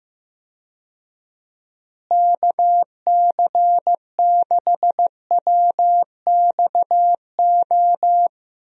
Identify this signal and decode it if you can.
Morse code, heard as a keyed tone, about 15 wpm, 700 Hz. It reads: KC6WXO
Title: USA Call Signs - 6 Characters - Repeated 2x - 15wpm